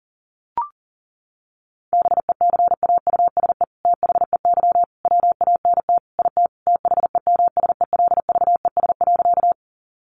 Morse code, heard as a keyed tone, about 40 wpm, 700 Hz. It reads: BECAUSE THEY WANT IT THEMSELVES.
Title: Sentences from Top 800 Words 40wpm